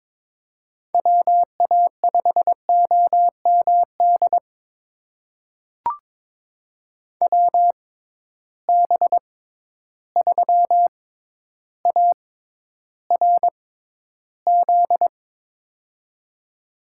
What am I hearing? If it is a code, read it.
Morse code, heard as a keyed tone, about 22 wpm, spaced as for 11 wpm, 700 Hz. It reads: WA5OMD WB3ARZ
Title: USA Call Signs - 6 Characters - 6x Character Spacing - 22wpm